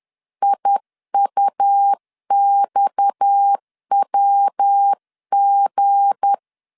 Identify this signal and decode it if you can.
Morse code, heard as a keyed tone, about 10 wpm, 800 Hz. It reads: IUXWG
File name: LEZIONI / 7 GF9Z / 7 G 10wpm 2501